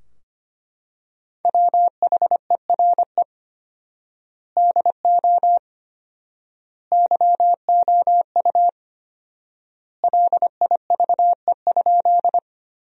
Morse code, heard as a keyed tone, about 25 wpm, 700 Hz. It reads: WHERE DO YOU LIVE?
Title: 4x Word-Spacing - Sentences from Top 200 Words - 25wpm